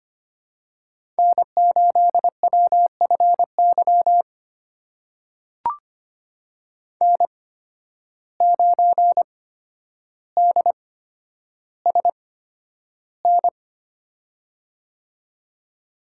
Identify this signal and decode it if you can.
Morse code, heard as a keyed tone, about 25 wpm, spaced as for 11 wpm, 700 Hz. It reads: N8WFY N9DSN
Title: USA Call Signs - 5 Characters - 8x Character Spacing - 25wpm